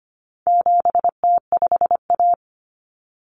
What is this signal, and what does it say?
Morse code, heard as a keyed tone, about 25 wpm, 700 Hz. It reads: 7T5A
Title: Common Cut Numbers - 4-Digit 25wpm